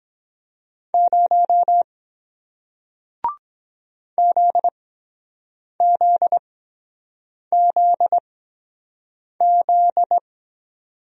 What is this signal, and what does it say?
Morse code, heard as a keyed tone, about 23 wpm, spaced as for 7 wpm, 700 Hz. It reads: 0 ZZZZ